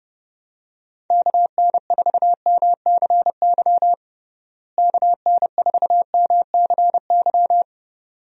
Morse code, heard as a keyed tone, about 31 wpm, 700 Hz. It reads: KN4MCY KN4MCY
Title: USA Call Signs - 6 Characters - Repeated 2x - 30wpm